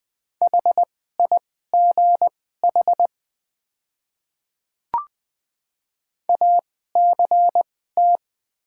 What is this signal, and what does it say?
Morse code, heard as a keyed tone, about 20 wpm, spaced as for 15 wpm, 700 Hz. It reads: HIGH ACT
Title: 2x Character Spacing - Top 200 Words - 20wpm